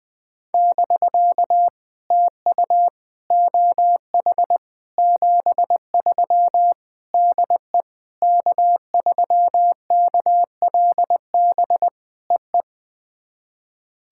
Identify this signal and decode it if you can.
Morse code, heard as a keyed tone, about 21 wpm, 700 Hz. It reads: <BK> TU OH 73 DE K3KLB EE
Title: POTA QSO Practice - 20wpm